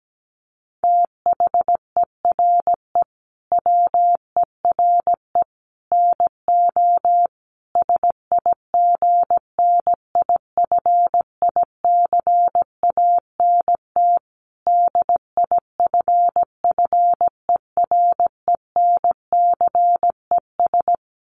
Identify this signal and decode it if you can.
Morse code, heard as a keyed tone, about 17 wpm, 700 Hz. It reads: THERE WERE NO SIGNIFICANT DIFFERENCES
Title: Sets of 5 Words Encore2 17wpm